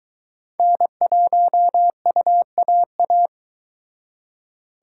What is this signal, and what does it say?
Morse code, heard as a keyed tone, about 23 wpm, 700 Hz. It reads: N1UAA